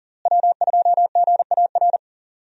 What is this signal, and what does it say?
Morse code, heard as a keyed tone, about 40 wpm, 700 Hz. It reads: W2GAR
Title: USA Call Signs - 5 Characters - Repeated 2x - 40wpm